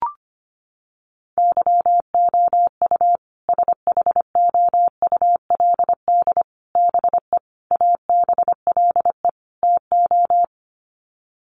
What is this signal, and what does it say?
Morse code, heard as a keyed tone, about 25 wpm, 700 Hz. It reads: YOU SHOULD BE ABLE TO
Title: Sets of 5 Words Encore 25wpm